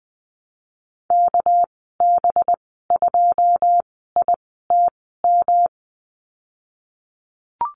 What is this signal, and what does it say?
Morse code, heard as a keyed tone, about 20 wpm, spaced as for 14 wpm, 700 Hz. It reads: KB2ITM
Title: 200 Lesson 32g - Letter k - Call Signs 20wpm